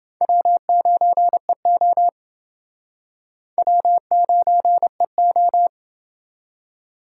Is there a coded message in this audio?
Yes: Morse code from 0.2 to 5.7 s, heard as a keyed tone, about 29 wpm, 700 Hz. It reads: W9EO W9EO